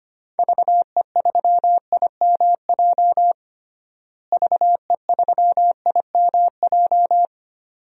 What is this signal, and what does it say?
Morse code, heard as a keyed tone, about 25 wpm, 700 Hz. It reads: VE3IMJ VE3IMJ
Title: DXCC-100 Call Signs - 6 Characters - Repeated 2x - 25wpm